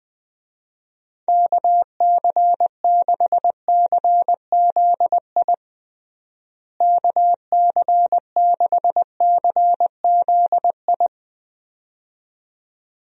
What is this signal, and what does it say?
Morse code, heard as a keyed tone, about 20 wpm, 700 Hz. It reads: KC6CZI KC6CZI
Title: USA Call Signs - 6 Characters - Repeated 2x - 20wpm